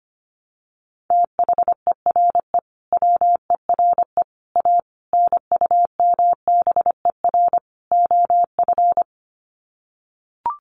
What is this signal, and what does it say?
Morse code, heard as a keyed tone, about 25 wpm, 700 Hz. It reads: THERE WERE A NUMBER OF
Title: Sets of 5 Words Encore2 25wpm